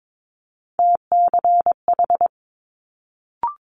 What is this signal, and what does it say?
Morse code, heard as a keyed tone, about 22 wpm, 700 Hz. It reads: TCH
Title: Multi-letter Phonemes Quick Fire 22wpm